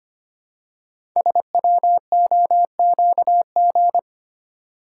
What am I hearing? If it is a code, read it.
Morse code, heard as a keyed tone, about 25 wpm, 700 Hz. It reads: SWOQG
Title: Code Groups - 5 Letters - 25wpm